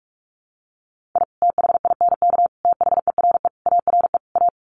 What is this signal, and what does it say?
Morse code, heard as a keyed tone, about 44 wpm, 700 Hz. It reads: I THINK THERE ARE A
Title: Sets of 5 Words Encore2 45wpm